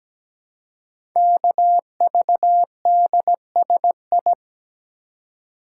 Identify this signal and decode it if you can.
Morse code, heard as a keyed tone, about 17 wpm, 700 Hz. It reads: KVDSI
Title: Code Groups - 5 Letters - 17wpm